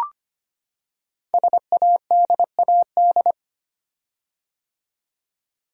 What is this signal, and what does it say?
Morse code, heard as a keyed tone, about 25 wpm, 700 Hz. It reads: SADAD
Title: Code Groups - 5 Letters - 25wpm